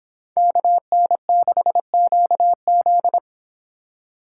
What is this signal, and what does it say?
Morse code, heard as a keyed tone, about 30 wpm, 700 Hz. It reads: KN6QZ